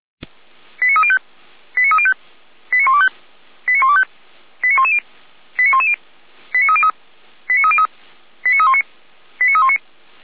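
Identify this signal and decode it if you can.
Five-tone selective call (ZVEI), heard as quick runs of five tones, tones of about 70 ms. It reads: 89305 89305 89125 89125 89110 89110 89303 89303 89319 89319
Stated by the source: Received on SDR, ZVEI five-tone selective call.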